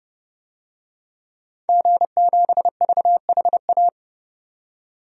Morse code, heard as a keyed tone, about 30 wpm, 700 Hz. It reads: G7VHA
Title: DXCC-100 Call Signs - 5 Characters - 30wpm